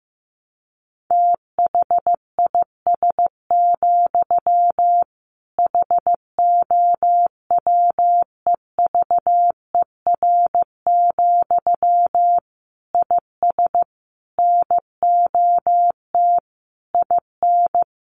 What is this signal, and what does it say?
Morse code, heard as a keyed tone, about 15 wpm, 700 Hz. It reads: THIS, HOWEVER, IS NOT IN
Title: Sentences from Top 1200 Words 15wpm